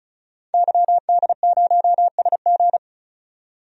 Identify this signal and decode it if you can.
Morse code, heard as a keyed tone, about 36 wpm, 700 Hz. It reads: YD0SG
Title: DXCC-100 Call Signs - 5 Characters - 35wpm